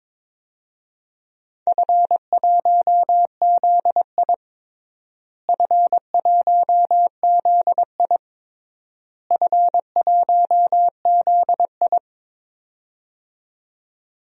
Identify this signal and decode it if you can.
Morse code, heard as a keyed tone, about 22 wpm, 700 Hz. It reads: F1ZI F1ZI F1ZI
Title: DXCC-100 Call Signs - 4 Characters - Repeated 3x - 22wpm